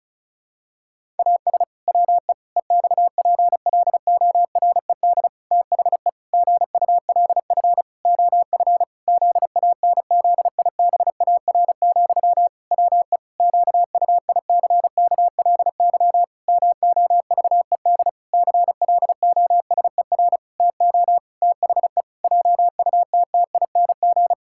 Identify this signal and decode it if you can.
Morse code, heard as a keyed tone, about 34 wpm, 700 Hz. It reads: AS WE EXPLORED THE GULF OF ZANZIBAR, WE QUICKLY MOVED CLOSER TO THE JUTTING
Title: Pangram Challenge v2 - 35wpm